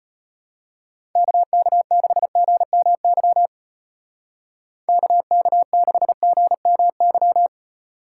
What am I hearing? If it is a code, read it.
Morse code, heard as a keyed tone, about 30 wpm, 700 Hz. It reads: KK6GMY KK6GMY
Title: Speed Racing - USA 2x3 Call Signs 25wpm